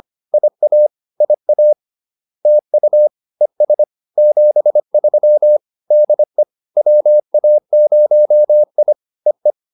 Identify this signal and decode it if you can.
Morse code, heard as a keyed tone, about 25 wpm, 600 Hz. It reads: IA IA TU ES 73 DE WA0I EE
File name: POTA107 25